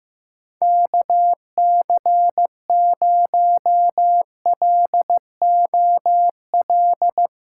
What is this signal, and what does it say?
Morse code, heard as a keyed tone, about 15 wpm, 700 Hz. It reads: KC0LOL
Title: USA Callsign DE Callsign 15wpm